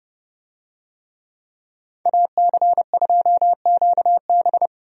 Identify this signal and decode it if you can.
Morse code, heard as a keyed tone, about 30 wpm, 700 Hz. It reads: AC2QB
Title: USA Call Signs - 5 Characters - Repeated 2x - 30wpm